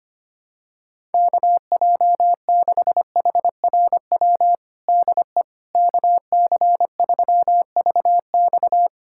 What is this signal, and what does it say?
Morse code, heard as a keyed tone, about 25 wpm, 700 Hz. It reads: KJ6HRW DE KC3VX
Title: USA Callsign DE Callsign 25wpm